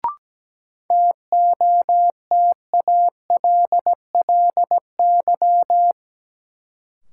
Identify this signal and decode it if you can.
Morse code, heard as a keyed tone, about 17 wpm, 700 Hz. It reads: TOTALLY
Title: Sentences from Top 1600 Words - New Word Drill Alt - 17wpm